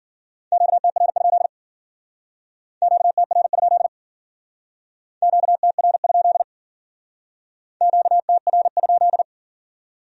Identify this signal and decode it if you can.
Morse code, heard as a keyed tone, about 48 wpm, 700 Hz. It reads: QTR? QTR? QTR? QTR?
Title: Speed-Racing Q-Codes - 40wpm